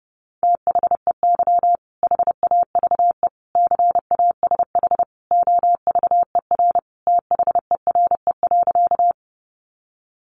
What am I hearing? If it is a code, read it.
Morse code, heard as a keyed tone, about 30 wpm, 700 Hz. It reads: THEY HAVE CASH OVER THERE.
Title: Sentences from Top 700 Words 30wpm